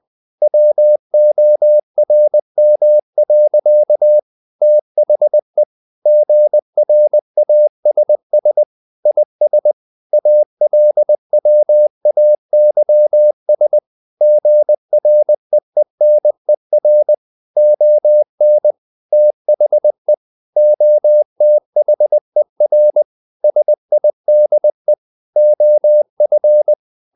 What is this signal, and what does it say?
Morse code, heard as a keyed tone, about 20 wpm, 600 Hz. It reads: WORM. THE GRASS IS ALWAYS GREENER ON THE OTHER SIDE OF
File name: CW / Practice / EnglishSayings20